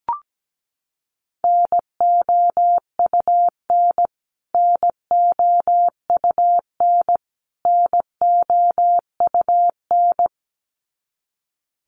Morse code, heard as a keyed tone, about 17 wpm, 700 Hz. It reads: NOUN NOUN NOUN
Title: Top 500 Words Repeated 17wpm